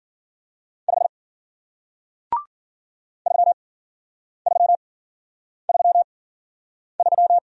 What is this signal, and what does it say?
Morse code, heard as a keyed tone, about 50 wpm, spaced as for 9 wpm, 700 Hz. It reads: 5 3333